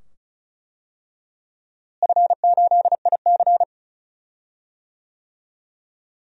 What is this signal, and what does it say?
Morse code, heard as a keyed tone, about 35 wpm, 700 Hz. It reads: F8IC
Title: DXCC-100 Call Signs - 4 Characters - Repeated 2x - 35wpm